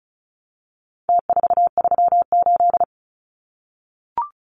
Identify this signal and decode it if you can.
Morse code, heard as a keyed tone, about 34 wpm, 700 Hz. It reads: T438
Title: Numbers - Contest Style - 4-Digit 35wpm